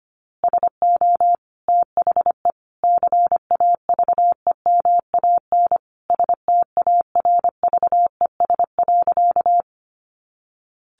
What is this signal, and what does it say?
Morse code, heard as a keyed tone, about 25 wpm, 700 Hz. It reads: SO THE CAVEMAN STARVES.
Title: A Sound Of Thunder 25wpm